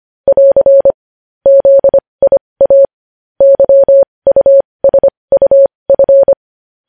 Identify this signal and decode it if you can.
Morse code, heard as a keyed tone, about 25 wpm, spaced as for 20 wpm, 550 Hz. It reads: <AR> ZIA YUSUF
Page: news-headlines-in-morse-code-at-20-wpm